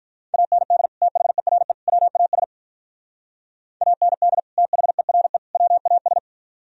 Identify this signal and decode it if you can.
Morse code, heard as a keyed tone, about 44 wpm, 700 Hz. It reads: AND THERE WAS AND THERE WAS
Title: Speed-Racing - Sets of 3 Words 35wpm